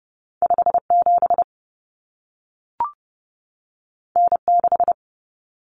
Morse code, heard as a keyed tone, about 30 wpm, 700 Hz. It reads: 57 N6
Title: Common Cut Numbers - 2-Digit 30wpm